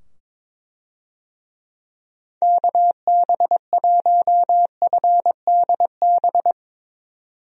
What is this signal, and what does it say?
Morse code, heard as a keyed tone, about 22 wpm, 700 Hz. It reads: KB1FDB
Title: USA Call Signs - 6 Characters - 22wpm